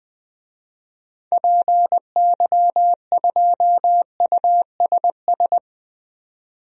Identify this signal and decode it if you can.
Morse code, heard as a keyed tone, about 20 wpm, 700 Hz. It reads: PY2USS
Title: DXCC-100 Call Signs - 6 Characters - 20wpm